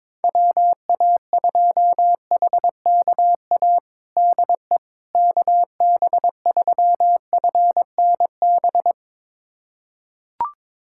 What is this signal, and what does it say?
Morse code, heard as a keyed tone, about 22 wpm, 700 Hz. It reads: WA2HKA DE KB3FNB
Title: USA Callsign DE Callsign - 6 Characters - 22wpm